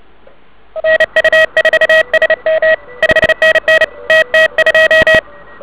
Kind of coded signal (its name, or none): Morse code